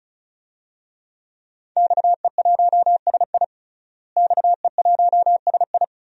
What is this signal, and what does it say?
Morse code, heard as a keyed tone, about 36 wpm, 700 Hz. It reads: XE1SI XE1SI
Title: DXCC-100 Call Signs - 5 Characters - Repeated 3x - 35wpm